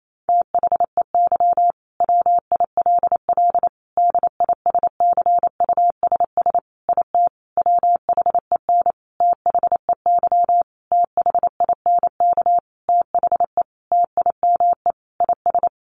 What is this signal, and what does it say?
Morse code, heard as a keyed tone, about 28 wpm, 700 Hz. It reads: THEY WILL DISCUSS IT WHEN THEY THINK THE TIME IS
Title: Sentences from Top 1400 Words 28wpm